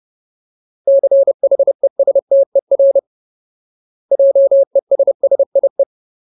Morse code, heard as a keyed tone, about 30 wpm, 550 Hz. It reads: CHESTER JESSIE
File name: Ultimate Word List / 30WPM Ultimate